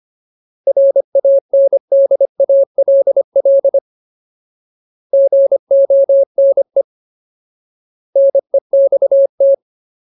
Morse code, heard as a keyed tone, about 25 wpm, 550 Hz. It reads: RANDALL GONE NEXT
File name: Ultimate Word List / 25WPM Ultimate